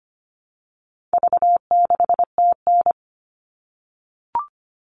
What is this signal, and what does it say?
Morse code, heard as a keyed tone, about 25 wpm, 700 Hz. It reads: V6TN
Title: All Cut Numbers - 4-Digit 25wpm